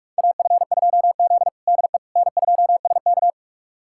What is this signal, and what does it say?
Morse code, heard as a keyed tone, about 47 wpm, 700 Hz. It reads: AF2Z DE N2SK
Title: USA Callsign DE Callsign - 4 Characters - 45wpm